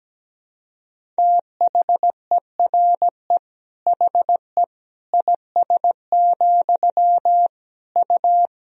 Morse code, heard as a keyed tone, about 17 wpm, 700 Hz. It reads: THERE HE IS, U
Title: Sentences from Top 100 Words and Abbreviations 17wpm